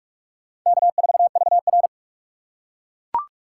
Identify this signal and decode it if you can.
Morse code, heard as a keyed tone, about 41 wpm, 700 Hz. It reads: K4VR